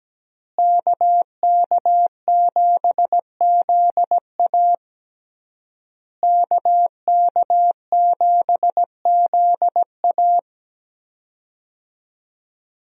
Morse code, heard as a keyed tone, about 17 wpm, 700 Hz. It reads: KK7ZA KK7ZA
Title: USA Call Signs - 5 Characters - Repeated 3x - 17wpm